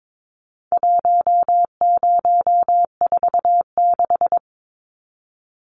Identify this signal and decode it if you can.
Morse code, heard as a keyed tone, about 22 wpm, 700 Hz. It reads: 1046
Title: Numbers - 4-Digit 22wpm